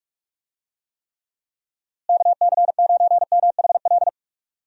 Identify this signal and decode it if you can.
Morse code, heard as a keyed tone, about 46 wpm, 700 Hz. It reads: KC9MHL
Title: USA Call Signs - 6 Characters - Repeated 2x - 45wpm